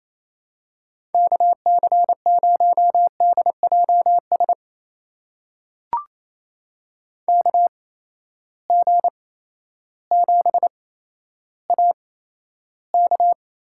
Morse code, heard as a keyed tone, about 28 wpm, spaced as for 13 wpm, 700 Hz. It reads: KC0DJS KG7AK
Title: USA Call Signs - 6 Characters - 8x Character Spacing - 28wpm